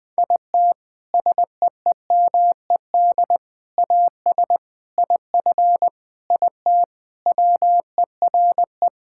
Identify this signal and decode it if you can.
Morse code, heard as a keyed tone, about 20 wpm, 700 Hz. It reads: IT SEEMED AS IF IT WERE
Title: Sentences with 4-Word Phrases - Spoken Beginning - 20wpm